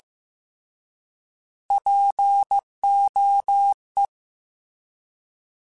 Morse code, heard as a keyed tone, about 15 wpm, 800 Hz. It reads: POE
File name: Alphabetical / Alphabetical mp3s / Ps @13